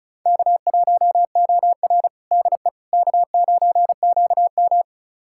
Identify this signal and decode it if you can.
Morse code, heard as a keyed tone, about 36 wpm, 700 Hz. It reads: K1OR DE K9QM
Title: USA Callsign DE Callsign - 4 Characters - 35wpm